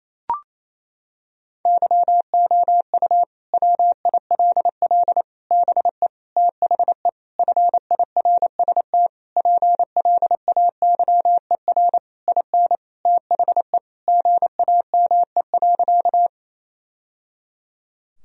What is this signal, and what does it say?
Morse code, heard as a keyed tone, about 28 wpm, 700 Hz. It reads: YOU WILL BE THE FIRST PLAYER IN THE GAME.
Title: Sentences from Top 1000 Words - Review Part 4 - 28wpm